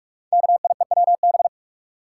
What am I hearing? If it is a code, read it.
Morse code, heard as a keyed tone, about 44 wpm, 690 Hz. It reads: KIEWB
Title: Code Groups - 5 Letters - 45wpm